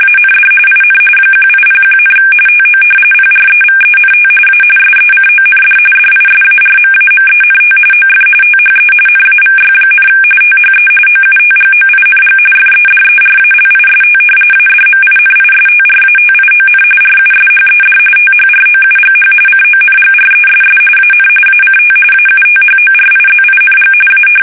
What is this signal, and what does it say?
RTTY, about 45.45 baud. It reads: AND IN 1801, TO PREVENT A SECOND DISARMAMENT OF THE NAVY, THE OUTGOING FEDERALIST ADMINISTRATION RUSHED THROUGH CONGRESS AN ACT AUTHORIZING A
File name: fsk850 / usnavy1